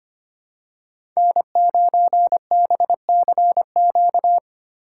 Morse code, heard as a keyed tone, about 25 wpm, 700 Hz. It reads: N9BCQ
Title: USA Call Signs - 5 Characters - 2x Character Spacing - 25wpm